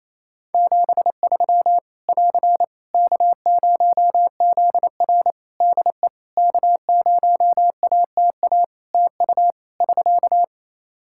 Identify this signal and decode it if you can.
Morse code, heard as a keyed tone, about 27 wpm, 700 Hz. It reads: 73 <AR> K0ZR DE K0ATA TU <SK>
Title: Practice QSO - Exchange 4 - Ending - 28wpm